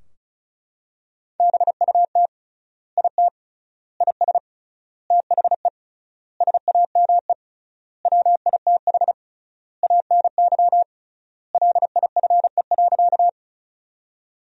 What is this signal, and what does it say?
Morse code, heard as a keyed tone, about 36 wpm, 700 Hz. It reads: BUT IT IS THE SAME WITH ANY LIFE.
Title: 3x Word-Spacing - Sentences from Top 300 Words - 35wpm